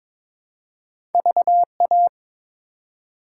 Morse code, heard as a keyed tone, about 21 wpm, 700 Hz. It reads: VA
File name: ARRL Sections v2 - 22wpm